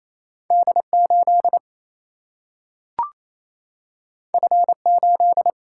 Morse code, heard as a keyed tone, about 28 wpm, 700 Hz. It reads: D8 F8
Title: Single-Letter Single-Number 28wpm